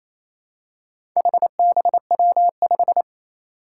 Morse code, heard as a keyed tone, about 27 wpm, 700 Hz. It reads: HBW5
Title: Four Challenging Letter-Number Combinations 28wpm